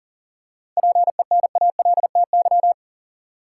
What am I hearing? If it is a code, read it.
Morse code, heard as a keyed tone, about 40 wpm, 700 Hz. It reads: PENALTY
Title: Seven Letter Words - 40wpm